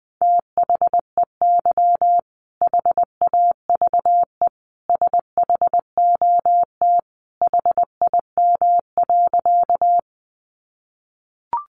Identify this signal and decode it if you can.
Morse code, heard as a keyed tone, about 20 wpm, 700 Hz. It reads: THEY HAVE SHOT HIM.
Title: Sentences from Top 1100 Words 20wpm